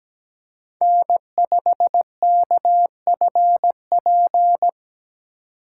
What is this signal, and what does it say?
Morse code, heard as a keyed tone, about 17 wpm, 700 Hz. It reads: N5KFP
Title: USA Call Signs - 5 Characters - 4x Character Spacing - 17wpm